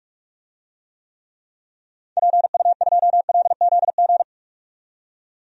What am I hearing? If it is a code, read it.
Morse code, heard as a keyed tone, about 46 wpm, 700 Hz. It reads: PU2LZG
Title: DXCC-100 Call Signs - 6 Characters - 45wpm